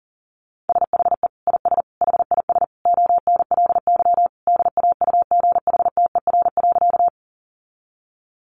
Morse code, heard as a keyed tone, about 40 wpm, 700 Hz. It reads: SHE IS HIS ONLY DAUGHTER.
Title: Sentences from Top 1500 Words 40wpm